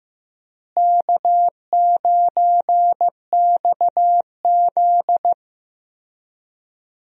Morse code, heard as a keyed tone, about 15 wpm, 700 Hz. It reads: K9XZ
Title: USA Call Signs - 4 Characters - Repeated 3x - 15wpm